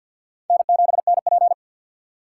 Morse code, heard as a keyed tone, about 50 wpm, 700 Hz. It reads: N7NP
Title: USA Call Signs - 4 Characters - 3x Character Spacing - 50wpm